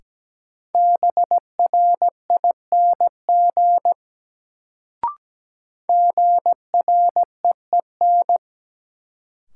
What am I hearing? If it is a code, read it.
Morse code, heard as a keyed tone, about 17 wpm, 700 Hz. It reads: BRING GREEN
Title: Sentences from Top 500 Words - New Word Drill - 17wpm